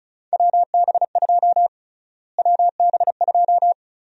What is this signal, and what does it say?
Morse code, heard as a keyed tone, about 36 wpm, 700 Hz. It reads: WB2 WB2
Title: USA Call Sign Prefixes - Repeated 2x - 35wpm